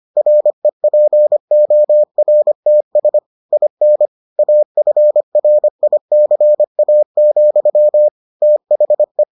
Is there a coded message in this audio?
Yes: Morse code from 0.2 to 9.2 s, heard as a keyed tone, about 25 wpm, 600 Hz. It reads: REPORTS IN AFRICA, THE